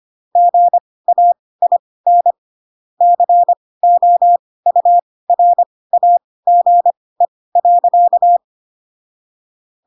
Morse code, heard as a keyed tone, about 25 wpm, spaced as for 18 wpm, 700 Hz. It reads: GAIN COURAGE.
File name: Pinocchio / Pinocchio ch1 12 25 ik1hge